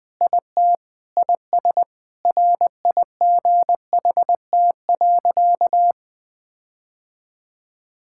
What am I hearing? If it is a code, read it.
Morse code, heard as a keyed tone, about 20 wpm, 700 Hz. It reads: IT IS RIGHT.
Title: Sentences with 4-Word Phrases - 20wpm